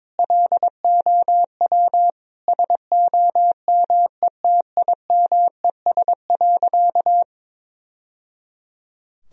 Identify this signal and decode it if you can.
Morse code, heard as a keyed tone, about 22 wpm, 700 Hz. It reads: LOW SOMETIMES.
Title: Sentences from Top 1000 Words - Review Part 3 - 22wpm